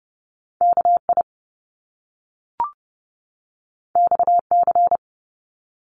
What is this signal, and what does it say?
Morse code, heard as a keyed tone, about 30 wpm, 700 Hz. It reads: KI XC
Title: Top Two Letter Combinations in Words 30wpm